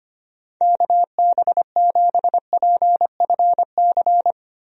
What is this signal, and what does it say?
Morse code, heard as a keyed tone, about 25 wpm, 700 Hz. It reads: KB7PFC
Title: USA Call Signs - 6 Characters - 25wpm